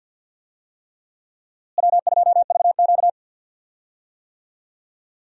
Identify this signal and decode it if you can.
Morse code, heard as a keyed tone, about 53 wpm, 700 Hz. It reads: W2VQ
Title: USA Call Signs - 4 Characters - Repeated 3x - 50wpm